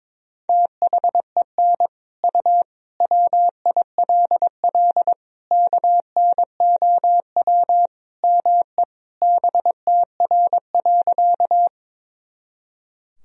Morse code, heard as a keyed tone, about 22 wpm, 700 Hz. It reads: THEN U WILL KNOW ME BTR.
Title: Sentences from Top 400 Words and Abbreviations 22wpm